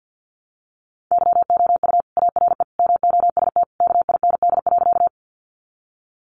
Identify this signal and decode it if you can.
Morse code, heard as a keyed tone, about 50 wpm, 700 Hz. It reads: YOU ARE MOST KIND.
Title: Sentences from Top 200 Words 50wpm